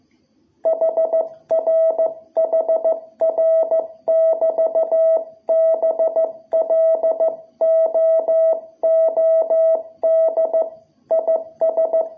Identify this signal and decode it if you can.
Morse code, heard as a keyed tone, about 15 wpm, 660 Hz. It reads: HR HR <BT> BLOOD IS
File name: jh8cbh / 2021cw